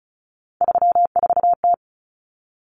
Morse code, heard as a keyed tone, about 34 wpm, 700 Hz. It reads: 34T
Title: Common Cut Numbers - 3-Digit 35wpm